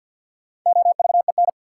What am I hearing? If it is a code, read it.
Morse code, heard as a keyed tone, about 50 wpm, 700 Hz. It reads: OVEN
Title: Sentences from Top 1200 Words - New Word Drill Alt - 50wpm